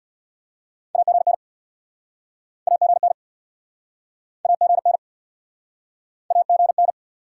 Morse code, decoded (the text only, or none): AGN AGN AGN AGN